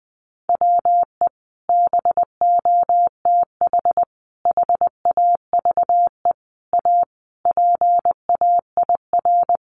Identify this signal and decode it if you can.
Morse code, heard as a keyed tone, about 20 wpm, 700 Hz. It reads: WE BOTH HAVE A PAIR
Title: Sentences from Top 1200 Words 20wpm